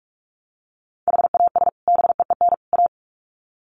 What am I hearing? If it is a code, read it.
Morse code, heard as a keyed tone, about 44 wpm, 700 Hz. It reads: HAS BEEN A
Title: Sets of 3 Words 45wpm